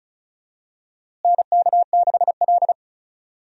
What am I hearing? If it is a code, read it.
Morse code, heard as a keyed tone, about 36 wpm, 700 Hz. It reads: NK6L
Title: USA Call Signs - 4 Characters - Repeated 2x - 35wpm